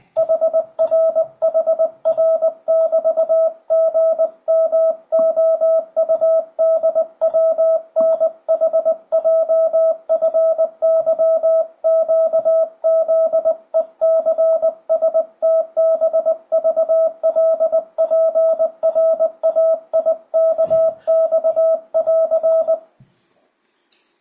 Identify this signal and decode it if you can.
Morse code, heard as a keyed tone, about 20 wpm, 650 Hz, with radio noise. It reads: HRHR<BT>GMOUDWNHJFYQZECSTBVLPRAIKX<AR>
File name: 2025cw / speedup / 20wpm